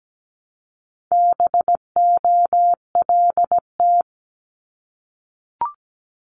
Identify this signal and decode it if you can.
Morse code, heard as a keyed tone, about 17 wpm, 700 Hz. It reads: BOLT
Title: Multi-letter Phonemes 17wpm